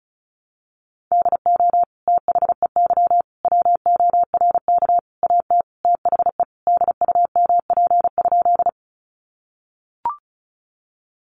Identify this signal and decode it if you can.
Morse code, heard as a keyed tone, about 34 wpm, 700 Hz. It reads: DO THEY WORK AT THE DUMP?
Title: Sentences from Top 1300 Words 35wpm